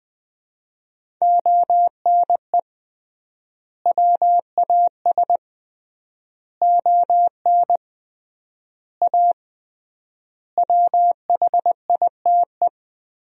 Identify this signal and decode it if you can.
Morse code, heard as a keyed tone, about 20 wpm, 700 Hz. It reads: ONE WAS ON A WHITE
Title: 3x Word-Spacing - Sentences from Top 300 Words - 20wpm